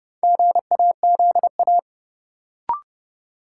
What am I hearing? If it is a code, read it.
Morse code, heard as a keyed tone, about 30 wpm, 700 Hz. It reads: GAZA
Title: Sets of 5 Words - New Word Drill - 30wpm